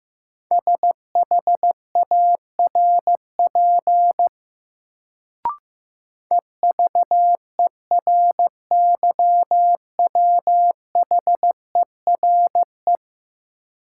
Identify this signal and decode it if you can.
Morse code, heard as a keyed tone, about 15 wpm, 700 Hz. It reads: SHARP EVERYWHERE
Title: Sentences from Top 1600 Words - New Word Drill Alt - 15wpm